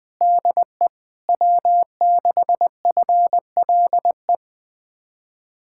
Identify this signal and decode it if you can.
Morse code, heard as a keyed tone, about 20 wpm, 700 Hz. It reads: DE W6FLE
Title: Speed-Racing Common QSO Phrases - 17wpm